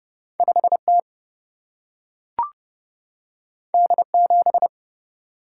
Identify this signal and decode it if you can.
Morse code, heard as a keyed tone, about 30 wpm, 700 Hz. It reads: 5T D7
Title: Single-Letter Single-Number 30wpm